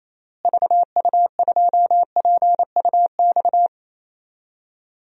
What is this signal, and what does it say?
Morse code, heard as a keyed tone, about 28 wpm, 700 Hz. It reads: VU2PUX
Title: DXCC-100 Call Signs - 6 Characters - Repeated 3x - 28wpm